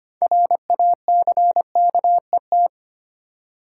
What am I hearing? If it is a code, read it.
Morse code, heard as a keyed tone, about 25 wpm, 700 Hz. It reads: RACKET
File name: practice / 25wpm / Easy Words with Hints - 25wpm